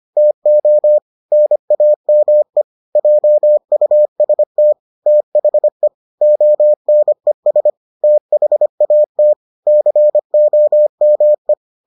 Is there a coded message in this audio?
Yes: Morse code from 0.2 to 11.5 s, heard as a keyed tone, about 25 wpm, 600 Hz. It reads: TO NAME JUST THE ONES THAT COME